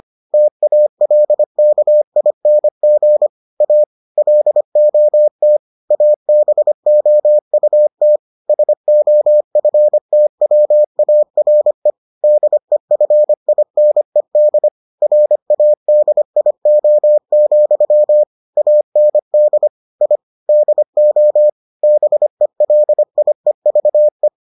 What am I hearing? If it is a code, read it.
Morse code, heard as a keyed tone, about 25 wpm, 600 Hz. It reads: TALKING A LOT ABOUT SOFTWARE DEFINED RADIO, AND I DO BELIEVE